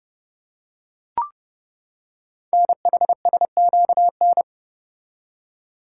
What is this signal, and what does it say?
Morse code, heard as a keyed tone, about 30 wpm, 700 Hz. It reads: NHSQN
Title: Code Groups - 5 Letter:Numbers - 30wpm